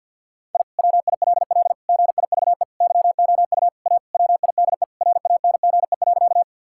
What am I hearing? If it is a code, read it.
Morse code, heard as a keyed tone, about 51 wpm, 700 Hz. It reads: I WILL GIVE YOU A WIDE RANGE.
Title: Sentences from Top 1000 Words - Review Part 6 - 50wpm